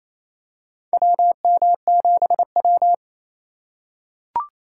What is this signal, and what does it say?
Morse code, heard as a keyed tone, about 28 wpm, 700 Hz. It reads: WM7W
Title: USA Call Signs - 3x Character Spacing - 28wpm